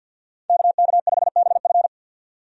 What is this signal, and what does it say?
Morse code, heard as a keyed tone, about 51 wpm, 700 Hz. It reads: KK5BF
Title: USA Call Signs - 5 Characters - 50wpm